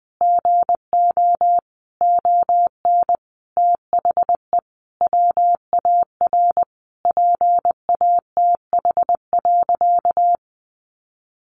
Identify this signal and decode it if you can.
Morse code, heard as a keyed tone, about 20 wpm, 700 Hz. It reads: GO ON THE WAR PATH.
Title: Sentences from Top 1400 Words 20wpm